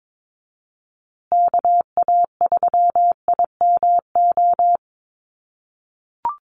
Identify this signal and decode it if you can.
Morse code, heard as a keyed tone, about 22 wpm, 700 Hz. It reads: KA3IMO
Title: USA Call Signs Repeated 3x 22wpm